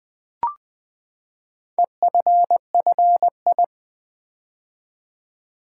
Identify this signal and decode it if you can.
Morse code, heard as a keyed tone, about 20 wpm, 700 Hz. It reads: EFFI
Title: Build A Word - 4 Letters At A Time - 10-Letter Words - 20wpm